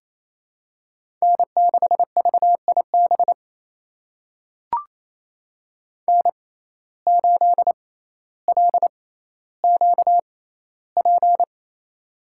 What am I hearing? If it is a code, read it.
Morse code, heard as a keyed tone, about 28 wpm, spaced as for 14 wpm, 700 Hz. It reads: N6VIB N8LQP
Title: USA Call Signs - 5 Characters - 6x Character Spacing - 28wpm